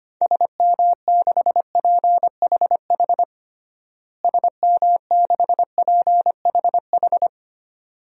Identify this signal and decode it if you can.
Morse code, heard as a keyed tone, about 25 wpm, 700 Hz. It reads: SM6PHH SM6PHH
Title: DXCC-100 Call Signs - 6 Characters - Repeated 2x - 25wpm